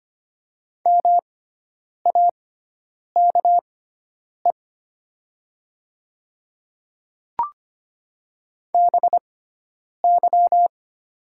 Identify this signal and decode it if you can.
Morse code, heard as a keyed tone, about 25 wpm, spaced as for 9 wpm, 700 Hz. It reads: MAKE BY
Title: 6x Character Spacing - Top 200 Words - 25wpm